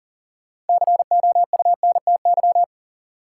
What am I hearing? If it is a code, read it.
Morse code, heard as a keyed tone, about 40 wpm, 700 Hz. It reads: COUNTY
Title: Sentences from Top 1200 Words - New Word Drill - 40wpm